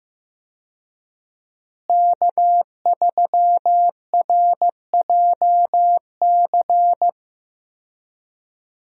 Morse code, heard as a keyed tone, about 15 wpm, 700 Hz. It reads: K3RJC
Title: USA Call Signs - 5 Characters - Repeated 2x - 15wpm